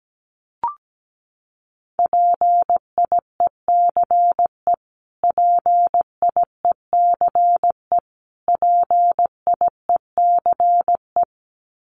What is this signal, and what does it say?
Morse code, heard as a keyed tone, about 17 wpm, 700 Hz. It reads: PIECE PIECE PIECE
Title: Top 500 Words Repeated 17wpm